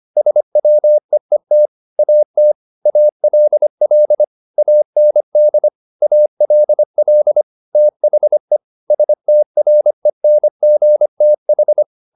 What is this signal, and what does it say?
Morse code, heard as a keyed tone, about 25 wpm, 600 Hz. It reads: SWEET AT ALL AND ALL THE STRENGTH
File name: the old man and the sea part 1 25wpm